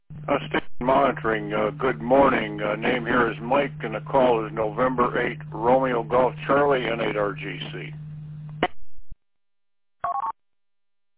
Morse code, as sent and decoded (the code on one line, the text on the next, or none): none